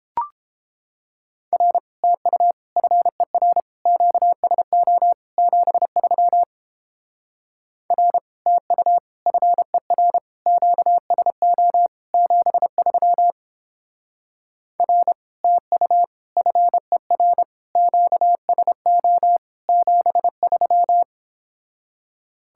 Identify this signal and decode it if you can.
Morse code, heard as a keyed tone, about 29 wpm, 700 Hz. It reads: R TU FER QSO 73 R TU FER QSO 73 R TU FER QSO 73